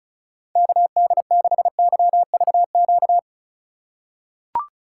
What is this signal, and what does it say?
Morse code, heard as a keyed tone, about 36 wpm, 700 Hz. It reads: KD6YVQ
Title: USA Call Signs - 6 Characters - 4x Character Spacing - 35wpm